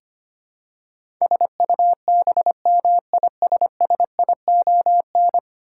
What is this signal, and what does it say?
Morse code, heard as a keyed tone, about 25 wpm, 700 Hz. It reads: SUBMISSION
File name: Ten Letter Words - 25wpm